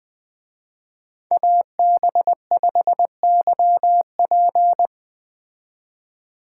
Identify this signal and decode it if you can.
Morse code, heard as a keyed tone, about 20 wpm, 700 Hz. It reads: AB5YP
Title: Speed Racing - USA 2x2 Call Signs 17wpm